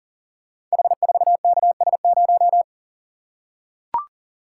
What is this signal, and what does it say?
Morse code, heard as a keyed tone, about 39 wpm, 700 Hz. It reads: H4KS0
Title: Code Groups - 5 Letter:Numbers - 40wpm